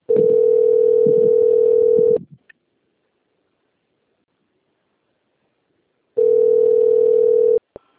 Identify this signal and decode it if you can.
Ringback tone (North America), 440 + 480 Hz.
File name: out- - - - - .37065